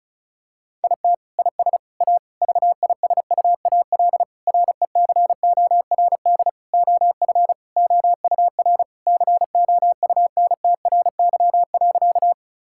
Morse code, heard as a keyed tone, about 34 wpm, 700 Hz. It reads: IT IS A VISUAL RECORD OF OUR COUNTRY.
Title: Sentences from Top 1000 Words - Review Part 6 - 35wpm